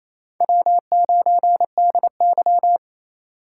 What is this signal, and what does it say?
Morse code, heard as a keyed tone, about 28 wpm, 700 Hz. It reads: W9DY
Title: K1USN SST QSO Practice - 28wpm